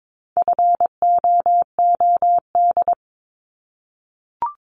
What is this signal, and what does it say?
Morse code, heard as a keyed tone, about 22 wpm, 700 Hz. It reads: FOOD
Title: Multi-letter Phonemes 22wpm